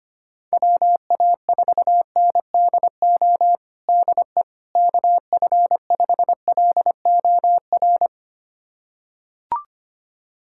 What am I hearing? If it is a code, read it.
Morse code, heard as a keyed tone, about 25 wpm, 700 Hz. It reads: WA4NDO DE KF5LOR
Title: USA Callsign DE Callsign 25wpm